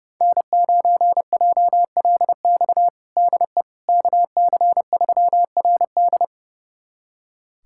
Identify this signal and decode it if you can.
Morse code, heard as a keyed tone, about 30 wpm, 700 Hz. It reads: N9JLX DE KC3RD
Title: USA Callsign DE Callsign 30wpm